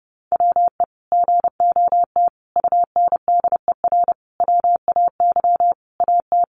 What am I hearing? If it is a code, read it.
Morse code, heard as a keyed tone, about 30 wpm, 700 Hz. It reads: WE GOT UNDER WAY AT
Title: Sentences from Top 300 Words 30wpm